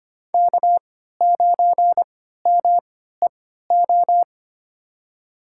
Morse code, heard as a keyed tone, about 25 wpm, spaced as for 14 wpm, 700 Hz. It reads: K9MEO
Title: USA Call Signs - 5 Characters - 3x Character Spacing - 25wpm